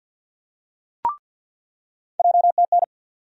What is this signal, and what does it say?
Morse code, heard as a keyed tone, about 48 wpm, 700 Hz. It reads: JTN
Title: USA Call Sign Postfixes - 3 Letters 50wpm